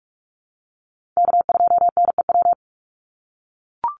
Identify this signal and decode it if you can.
Morse code, heard as a keyed tone, about 44 wpm, 700 Hz. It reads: K2NEW
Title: USA Call Signs 45wpm